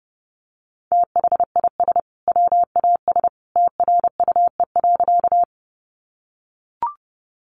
Morse code, heard as a keyed tone, about 30 wpm, 700 Hz. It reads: THIS WAS TRUE.
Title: Sentences from Top 400 Words 30wpm